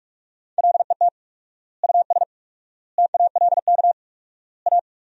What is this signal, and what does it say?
Morse code, heard as a keyed tone, about 42 wpm, 700 Hz. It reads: LET US TALK A
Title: 4x Word-Spacing - Sentences from Top 400 Words - 45wpm